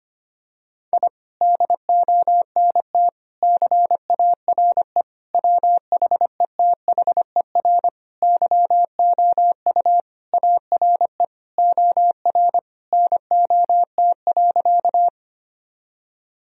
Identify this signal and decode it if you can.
Morse code, heard as a keyed tone, about 25 wpm, 700 Hz. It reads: I DONT CARE WHETHER YOU ARE OR NOT.
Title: Sentences from Top 1000 Words - Review Part 3 - 25wpm